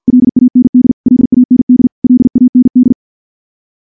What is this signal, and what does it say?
Morse code, heard as a keyed tone, about 47 wpm, 250 Hz. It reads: LAND LAND LAND
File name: morse / morse-training-50WPM